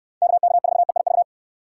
Morse code, heard as a keyed tone, about 64 wpm, 700 Hz. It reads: KK4EIV